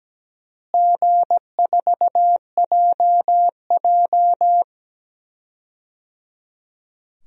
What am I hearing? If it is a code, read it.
Morse code, heard as a keyed tone, about 17 wpm, 700 Hz. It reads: G4JJ
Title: DXCC-100 Call Signs - 4 Characters - Repeated 3x - 17wpm